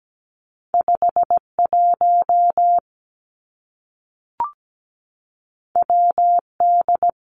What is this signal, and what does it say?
Morse code, heard as a keyed tone, about 17 wpm, 700 Hz. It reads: 51 WD
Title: Two Challenging Letter-Number Combinations 17wpm